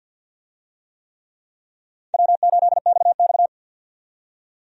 Morse code, heard as a keyed tone, about 48 wpm, 700 Hz. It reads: W8XX
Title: USA Call Signs - 4 Characters - 50wpm